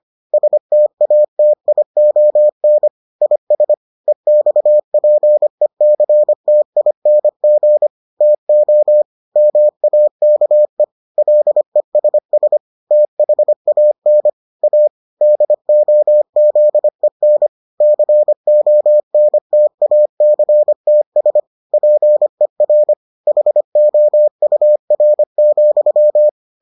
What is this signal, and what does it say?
Morse code, heard as a keyed tone, about 25 wpm, 600 Hz. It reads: STATION IS EXPECTING TO MAKE LESS THAN A DOZEN CONTACTS PER HOUR,